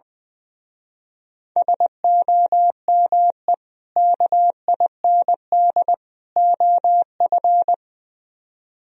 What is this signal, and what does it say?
Morse code, heard as a keyed tone, about 21 wpm, 700 Hz. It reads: SOME KIND OF
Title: Speed-Racing - Sets of 3 Words 20wpm